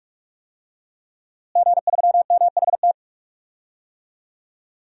Morse code, heard as a keyed tone, about 46 wpm, 700 Hz. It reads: G3MHT
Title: DXCC-100 Call Signs - 5 Characters - Repeated 3x - 45wpm